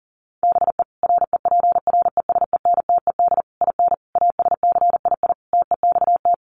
Morse code, heard as a keyed tone, about 37 wpm, 700 Hz. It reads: BE REPRESENTED IN ASCII TEXT
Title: Sets of 5 Words 40wpm